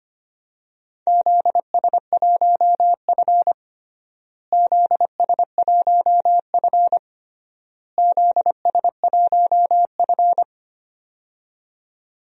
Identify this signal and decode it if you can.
Morse code, heard as a keyed tone, about 25 wpm, 700 Hz. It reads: ZS1F ZS1F ZS1F
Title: DXCC-100 Call Signs - 4 Characters - Repeated 3x - 25wpm